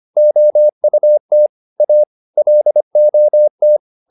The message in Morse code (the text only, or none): OUT A LOT